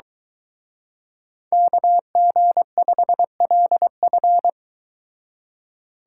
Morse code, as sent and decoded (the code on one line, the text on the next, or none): -.- --. ..... .-.. ..-.
KG5LF